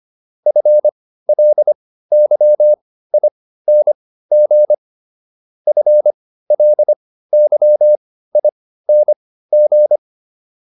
Morse code, heard as a keyed tone, about 25 wpm, spaced as for 15 wpm, 600 Hz. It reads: FLYING FLYING